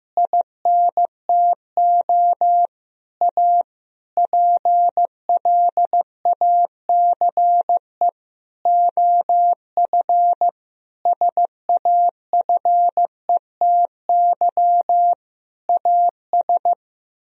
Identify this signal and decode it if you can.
Morse code, heard as a keyed tone, about 15 wpm, 700 Hz. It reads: INTO A PLACE OF SAFETY AS
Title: Sentences from Top 1000 Words - Review Part 4 - 15wpm